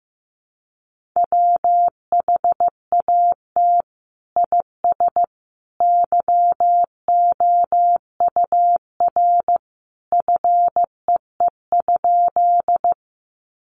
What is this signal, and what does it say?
Morse code, heard as a keyed tone, about 15 wpm, 700 Hz. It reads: WHAT IS YOUR FEE?
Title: Sentences from Top 1500 Words 15wpm